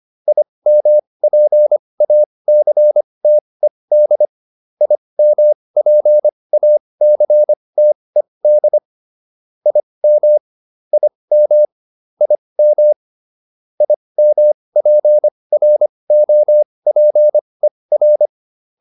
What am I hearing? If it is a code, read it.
Morse code, heard as a keyed tone, about 25 wpm, spaced as for 20 wpm, 600 Hz. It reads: IMPACTED IMPACTED IM IM IM IMPROPER
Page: /advanced-practice-files/